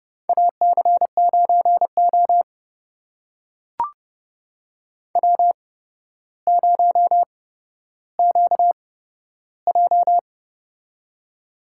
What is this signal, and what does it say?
Morse code, heard as a keyed tone, about 30 wpm, spaced as for 13 wpm, 700 Hz. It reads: AC9O W0QJ
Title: USA Call Signs - 4 Characters - 8x Character Spacing - 30wpm